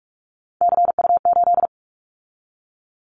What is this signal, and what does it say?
Morse code, heard as a keyed tone, about 46 wpm, 700 Hz. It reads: CU8
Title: Callsign Prefixes for Top 100 DXCC - 45wpm